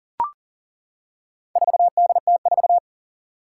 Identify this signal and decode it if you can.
Morse code, heard as a keyed tone, about 40 wpm, 700 Hz. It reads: 4DT4
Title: Four Challenging Letter-Number Combinations 40wpm